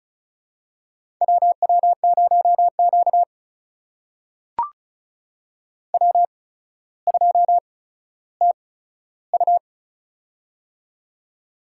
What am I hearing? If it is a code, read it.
Morse code, heard as a keyed tone, about 35 wpm, spaced as for 15 wpm, 700 Hz. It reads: WW0Q W2TU
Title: USA Call Signs - 4 Characters - 8x Character Spacing - 35wpm